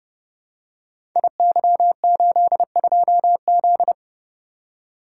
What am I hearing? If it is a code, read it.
Morse code, heard as a keyed tone, about 28 wpm, 700 Hz. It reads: IY82Z
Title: Code Groups - 5 Letter:Numbers - 30wpm